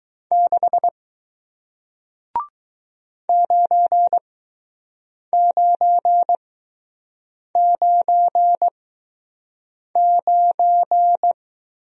Morse code, heard as a keyed tone, about 21 wpm, spaced as for 6 wpm, 700 Hz. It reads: 6 9999